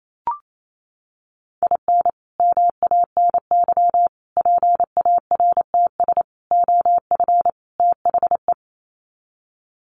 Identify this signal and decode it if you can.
Morse code, heard as a keyed tone, about 28 wpm, 700 Hz. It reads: IN MANY PARTS OF THE
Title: Sets of 5 Words Encore 28wpm